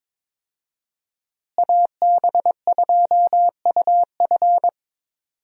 Morse code, heard as a keyed tone, about 22 wpm, 700 Hz. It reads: AB2UF
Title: USA Call Signs - 5 Characters - Repeated 2x - 22wpm